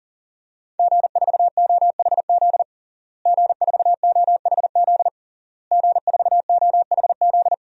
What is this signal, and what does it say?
Morse code, heard as a keyed tone, about 40 wpm, 700 Hz. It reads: G4OHZ G4OHZ G4OHZ
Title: DXCC-100 Call Signs - 5 Characters - Repeated 3x - 40wpm